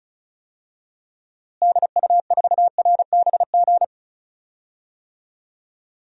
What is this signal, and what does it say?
Morse code, heard as a keyed tone, about 36 wpm, 700 Hz. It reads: DU4RBG
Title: DXCC-100 Call Signs - 6 Characters - Repeated 3x - 35wpm